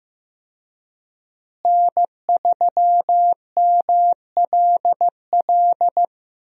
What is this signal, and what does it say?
Morse code, heard as a keyed tone, about 15 wpm, 700 Hz. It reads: N3MLL
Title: USA Call Signs - 5 Characters - Repeated 3x - 15wpm